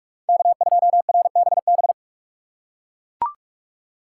Morse code, heard as a keyed tone, about 40 wpm, 700 Hz. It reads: K2RBB